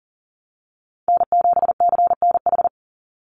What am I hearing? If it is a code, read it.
Morse code, heard as a keyed tone, about 40 wpm, 700 Hz. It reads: N7CNH
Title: NA SOTA Chaser Callsigns 40wpm